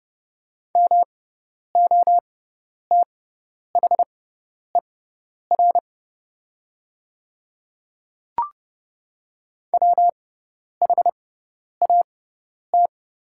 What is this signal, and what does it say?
Morse code, heard as a keyed tone, about 30 wpm, spaced as for 10 wpm, 700 Hz. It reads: MOTHER WHAT
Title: 6x Character Spacing - Top 200 Words - 30wpm